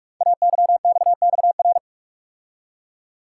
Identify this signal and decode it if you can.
Morse code, heard as a keyed tone, about 44 wpm, 700 Hz. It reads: AYXXR
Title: Code Groups - 5 Letters - 45wpm